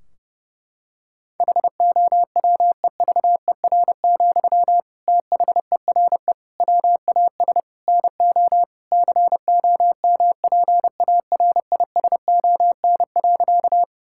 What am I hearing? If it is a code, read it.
Morse code, heard as a keyed tone, about 30 wpm, 700 Hz. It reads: HOWEVER, THERE WAS NO COMPARISON.
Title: Sentences from Top 1600 Words 30wpm